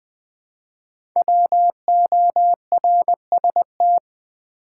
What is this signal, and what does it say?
Morse code, heard as a keyed tone, about 20 wpm, 700 Hz. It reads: WORST
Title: 5 Letter Words - First 3 Letters Given - 20wpm